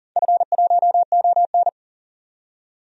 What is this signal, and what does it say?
Morse code, heard as a keyed tone, about 38 wpm, 700 Hz. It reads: F1ON
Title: DXCC-100 Call Signs - 4 Characters - 40wpm